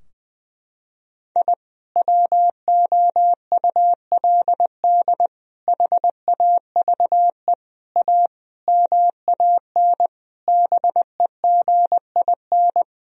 Morse code, heard as a keyed tone, about 20 wpm, 700 Hz. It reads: I WOULD HAVE A MAN BEGIN
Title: Sentences from Top 1000 Words - Review Part 2 - 20wpm